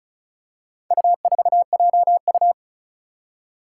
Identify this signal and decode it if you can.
Morse code, heard as a keyed tone, about 36 wpm, 700 Hz. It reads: U4JU
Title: Four Challenging Letter-Number Combinations 35wpm